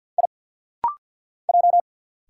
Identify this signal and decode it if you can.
Morse code, heard as a keyed tone, about 50 wpm, 700 Hz. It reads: I J
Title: Single Letters - Rapid-Fire 50wpm